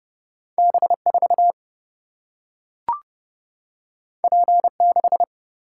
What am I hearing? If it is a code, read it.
Morse code, heard as a keyed tone, about 30 wpm, 700 Hz. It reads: B4 P6
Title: Single-Letter Single-Number 30wpm